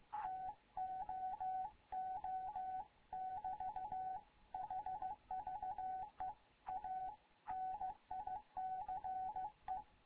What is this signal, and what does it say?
Morse code, heard as a keyed tone, about 15 wpm, 750 Hz, with radio noise. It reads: TOO <BT> HVE A NICE
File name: Morze-HA2MN / hangp / 2 11 GYAKORLAT